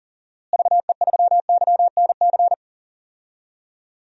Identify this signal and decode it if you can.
Morse code, heard as a keyed tone, about 40 wpm, 700 Hz. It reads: VE3YNC
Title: DXCC-100 Call Signs - 6 Characters - Repeated 2x - 40wpm